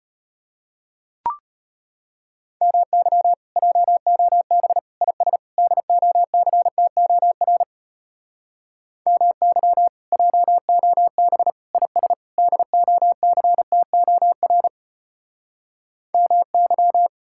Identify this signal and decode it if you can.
Morse code, heard as a keyed tone, about 33 wpm, 700 Hz. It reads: MY JOB IS DOCTOR MY JOB IS DOCTOR MY